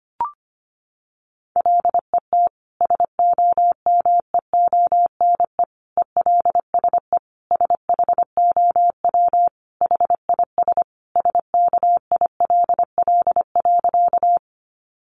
Morse code, heard as a keyed tone, about 25 wpm, 700 Hz. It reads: LET SOMEONE ELSE SHOW HIS SKILL.
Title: Sentences from Top 1200 Words 25wpm